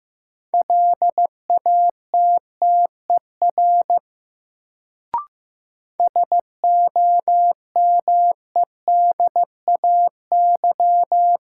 Morse code, heard as a keyed tone, about 15 wpm, 700 Hz. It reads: LATTER SOMEDAY
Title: Sentences from Top 1600 Words - New Word Drill - 15wpm